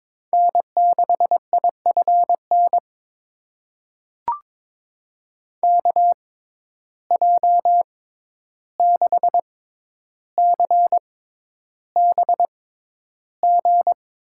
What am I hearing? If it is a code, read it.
Morse code, heard as a keyed tone, about 22 wpm, spaced as for 11 wpm, 700 Hz. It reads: N6IFN KJ6CBG
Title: USA Call Signs - 6x Character Spacing - 22wpm